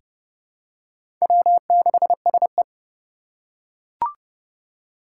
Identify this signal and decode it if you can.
Morse code, heard as a keyed tone, about 30 wpm, 700 Hz. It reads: W6SE
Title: USA Call Signs - 4 Characters - 4x Character Spacing - 30wpm